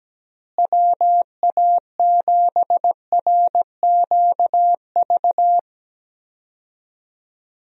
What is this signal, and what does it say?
Morse code, heard as a keyed tone, about 17 wpm, 700 Hz. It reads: WA7RQV
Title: USA Call Signs - 6 Characters - Repeated 3x - 17wpm